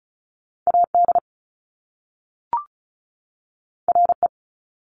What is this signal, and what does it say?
Morse code, heard as a keyed tone, about 36 wpm, 700 Hz. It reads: AD RE
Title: Top Two Letter Combinations in Words 35wpm